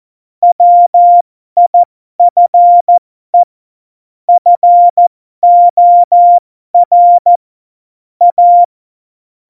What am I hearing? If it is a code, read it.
Morse code, heard as a keyed tone, about 14 wpm, spaced as for 12 wpm, 700 Hz. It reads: WIFE FOR A
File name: Morsekurs / Witze